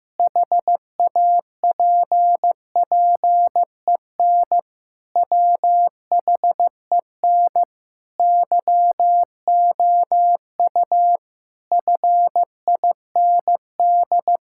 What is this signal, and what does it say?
Morse code, heard as a keyed tone, about 15 wpm, 700 Hz. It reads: HAPPEN WHEN YOU FIND
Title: Sentences from Top 1000 Words - Review Part 5 - 15wpm